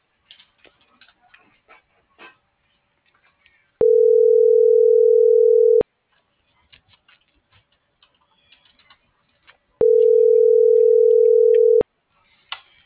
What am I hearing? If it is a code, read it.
Ringback tone (North America), 440 + 480 Hz.